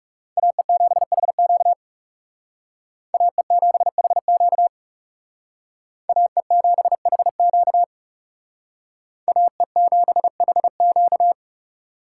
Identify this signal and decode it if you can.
Morse code, heard as a keyed tone, about 36 wpm, 700 Hz. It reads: AE7HQ AE7HQ AE7HQ AE7HQ
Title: Speed Racing - USA 2x2 Call Signs 30wpm